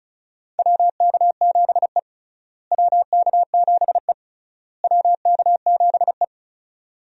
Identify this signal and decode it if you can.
Morse code, heard as a keyed tone, about 34 wpm, 700 Hz. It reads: WK7E WK7E WK7E
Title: USA Call Signs - 4 Characters - Repeated 3x - 35wpm